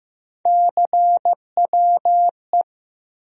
Morse code, heard as a keyed tone, about 15 wpm, 700 Hz. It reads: CWE
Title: USA Call Sign Postfixes - 3 Letters 15wpm